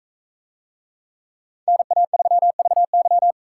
Morse code, heard as a keyed tone, about 40 wpm, 700 Hz. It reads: NA3VY